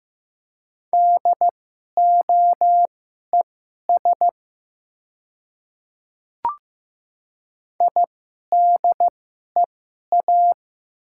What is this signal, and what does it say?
Morse code, heard as a keyed tone, about 15 wpm, spaced as for 11 wpm, 690 Hz. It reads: DOES IDEA
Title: 2x Character Spacing - Top 500 Words - 15wpm